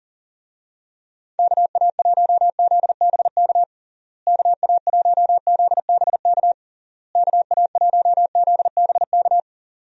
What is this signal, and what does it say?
Morse code, heard as a keyed tone, about 37 wpm, 700 Hz. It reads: KA1ZBK KA1ZBK KA1ZBK
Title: USA Call Signs - 6 Characters - Repeated 3x - 40wpm